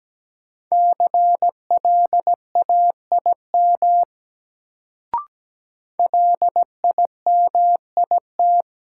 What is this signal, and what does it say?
Morse code, heard as a keyed tone, about 17 wpm, 700 Hz. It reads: CLAIM LIMIT
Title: Sentences from Top 1500 Words - New Word Drill Alt - 17wpm